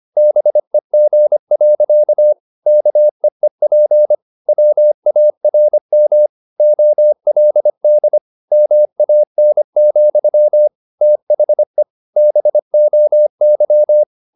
Morse code, heard as a keyed tone, about 25 wpm, 600 Hz. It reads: BEG. KEEP WARM OLD MAN, THE BOY
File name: the old man and the sea part 1 25wpm